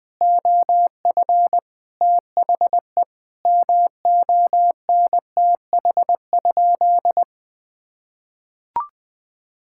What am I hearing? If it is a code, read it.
Morse code, heard as a keyed tone, about 20 wpm, 700 Hz. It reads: OF THE MONTH?
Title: Sentences from Top 1000 Words - Review Part 6 - 20wpm